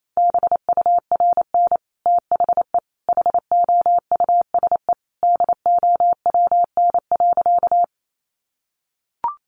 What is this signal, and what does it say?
Morse code, heard as a keyed tone, about 28 wpm, 700 Hz. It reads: BURN THE HOUSE DOWN.
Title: Sentences from Top 1300 Words 28wpm